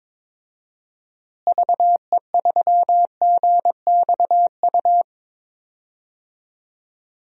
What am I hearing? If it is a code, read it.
Morse code, heard as a keyed tone, about 22 wpm, 700 Hz. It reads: VE3GXU
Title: DXCC-100 Call Signs - 6 Characters - Repeated 2x - 22wpm